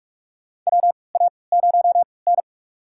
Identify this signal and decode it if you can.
Morse code, heard as a keyed tone, about 46 wpm, spaced as for 25 wpm, 700 Hz. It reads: WA0N
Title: USA Call Signs - 4 Characters - 3x Character Spacing - 45wpm